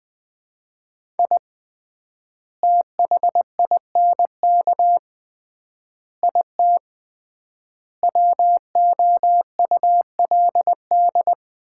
Morse code, heard as a keyed tone, about 21 wpm, 700 Hz. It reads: I THINK IT WOULD
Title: 3x Word-Spacing - Sets of 5 Words - 20wpm